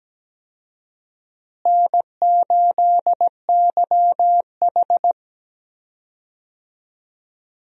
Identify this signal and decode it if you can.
Morse code, heard as a keyed tone, about 17 wpm, 700 Hz. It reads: N8YH
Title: USA Call Signs - 4 Characters - Repeated 2x - 17wpm